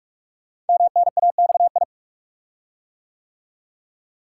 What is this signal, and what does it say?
Morse code, heard as a keyed tone, about 44 wpm, 690 Hz. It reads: MNAXI
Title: Code Groups - 5 Letters - 45wpm